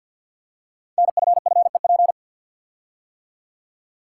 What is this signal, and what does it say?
Morse code, heard as a keyed tone, about 50 wpm, 690 Hz. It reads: NFFEP
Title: Code Groups - 5 Letters - 50wpm